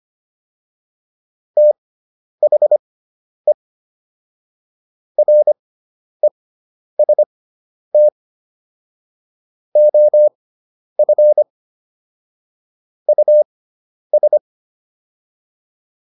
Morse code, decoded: THE REST OF US

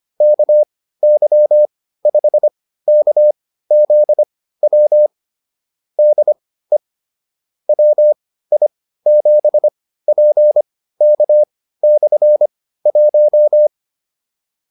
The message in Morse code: KY5KZW DE WI7PK/1

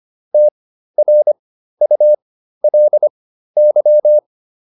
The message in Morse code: TRULY

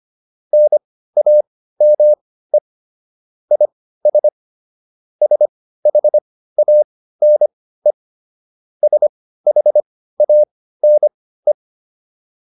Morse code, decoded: NAME IS SHANE SHANE

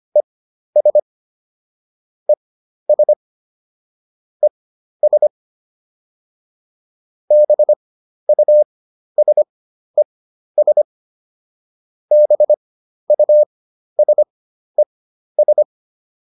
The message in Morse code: ES ES ES BUSES BUSES

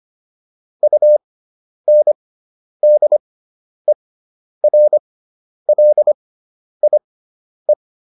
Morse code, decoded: UNDERLIE

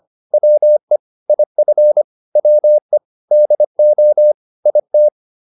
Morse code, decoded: WE IF WE DO IT

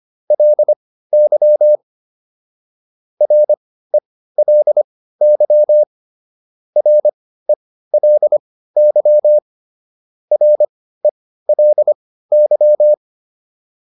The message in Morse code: LY RELY RELY RELY

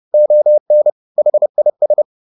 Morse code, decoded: ON HIS

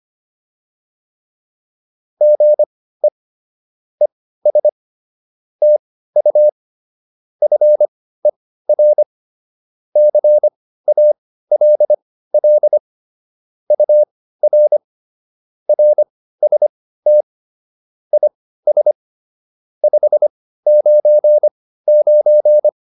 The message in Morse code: GE ES TU FER CALL UR RST IS 599

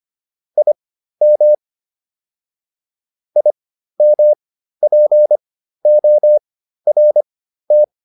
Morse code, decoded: IM IMPORT

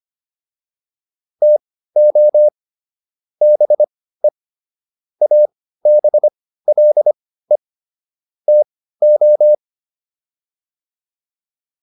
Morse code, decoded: TO BE ABLE TO

